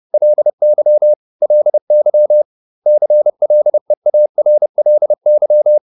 Morse code, heard as a keyed tone, about 30 wpm, 600 Hz. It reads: LY LY CLEARLY